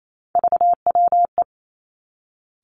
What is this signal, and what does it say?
Morse code, heard as a keyed tone, about 28 wpm, 700 Hz. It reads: VWE
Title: Three Challenging Letter-Number Combinations 28wpm